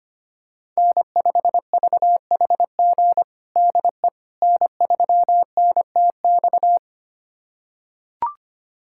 Morse code, heard as a keyed tone, about 25 wpm, 700 Hz. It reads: N5VHG DE N3NTX
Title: USA Callsign DE Callsign - 5 Characters - 25wpm